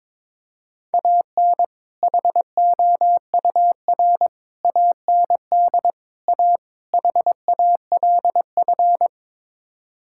Morse code, heard as a keyed tone, about 22 wpm, 700 Hz. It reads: AN HOUR AND A HALF
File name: practice / 22wpm / Build A Phrase - 5-Word Phrases - 22wpm